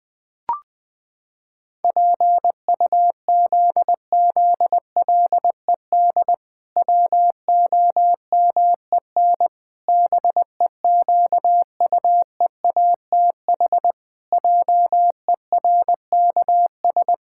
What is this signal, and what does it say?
Morse code, heard as a keyed tone, about 20 wpm, 700 Hz. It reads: PUZZLED WOMEN BEQUEATH JERKS
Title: Pangram Challenge v2 - 20wpm